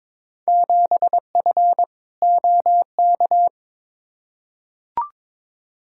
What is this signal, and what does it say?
Morse code, heard as a keyed tone, about 22 wpm, 700 Hz. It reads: 7F OK
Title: Field Day Exchanges Encore - 22wpm